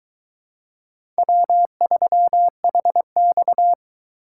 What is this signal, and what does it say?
Morse code, heard as a keyed tone, about 23 wpm, 700 Hz. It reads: W3HX